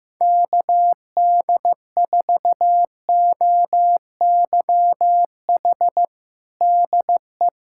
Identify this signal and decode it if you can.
Morse code, heard as a keyed tone, about 15 wpm, 700 Hz. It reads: KD4OYH DE
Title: USA Callsign DE Callsign - 6 Characters - 15wpm